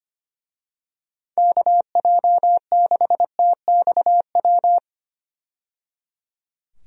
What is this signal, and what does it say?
Morse code, heard as a keyed tone, about 25 wpm, 700 Hz. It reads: KJ6TXW
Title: USA Call Signs - 6 Characters - 25wpm